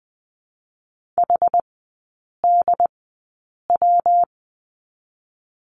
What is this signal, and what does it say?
Morse code, heard as a keyed tone, about 20 wpm, 700 Hz. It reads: H D W
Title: 114 Lesson 19e - Letter w - Three Characters 20wpm